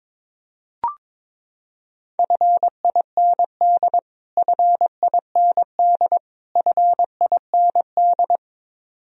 Morse code, heard as a keyed tone, about 22 wpm, 700 Hz. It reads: FIND FIND FIND
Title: Top 100 Words Repeated 22wpm